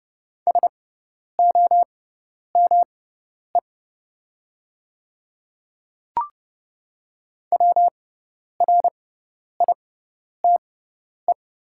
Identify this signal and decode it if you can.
Morse code, heard as a keyed tone, about 30 wpm, spaced as for 10 wpm, 700 Hz. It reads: SOME WRITE
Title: 6x Character Spacing - Top 100 Words - 30wpm